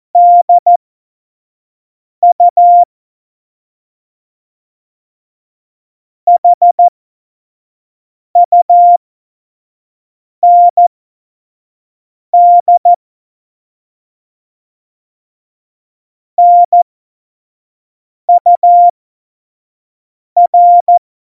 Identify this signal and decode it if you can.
Morse code, heard as a keyed tone, about 14 wpm, spaced as for 5 wpm, 700 Hz. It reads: DU HUND NUR
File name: Morsekurs / Witze / Witz 30. April 2019, 25 F70